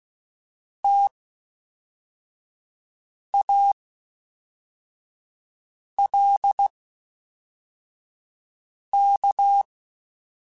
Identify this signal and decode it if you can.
Morse code, heard as a keyed tone, about 16 wpm, spaced as for 4 wpm, 800 Hz. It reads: TALK
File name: Misc / Misc-mp3s / WordPairs sp @5